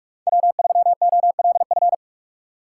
Morse code, heard as a keyed tone, about 39 wpm, 700 Hz. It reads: W3OLF